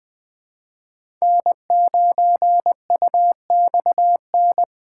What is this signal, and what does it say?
Morse code, heard as a keyed tone, about 21 wpm, 700 Hz. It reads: N9UXN
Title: USA Call Signs - 5 Characters - 3x Character Spacing - 20wpm